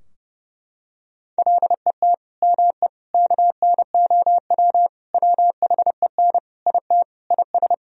Morse code, heard as a keyed tone, about 30 wpm, 700 Hz. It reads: LET ME KNOW WHEN IT IS
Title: Sentences from Top 1000 Words - Review Part 2 - 30wpm